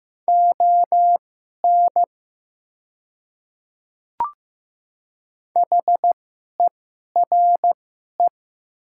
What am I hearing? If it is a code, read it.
Morse code, heard as a keyed tone, about 15 wpm, spaced as for 11 wpm, 700 Hz. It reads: ON HERE
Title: 2x Character Spacing - Top 200 Words - 15wpm